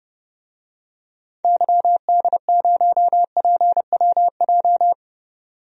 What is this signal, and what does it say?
Morse code, heard as a keyed tone, about 31 wpm, 700 Hz. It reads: YD0PWJ
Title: DXCC-100 Call Signs - 6 Characters - 30wpm